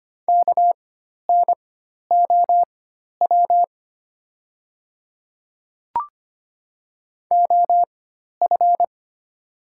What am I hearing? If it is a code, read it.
Morse code, heard as a keyed tone, about 25 wpm, spaced as for 12 wpm, 700 Hz. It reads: KNOW OF
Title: 4x Character Spacing - Top 200 Words - 25wpm